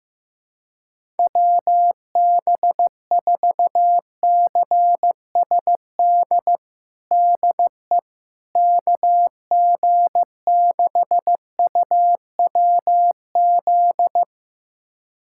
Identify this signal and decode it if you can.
Morse code, heard as a keyed tone, about 15 wpm, 700 Hz. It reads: WB4CSD DE KG6UWZ
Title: USA Callsign DE Callsign 15wpm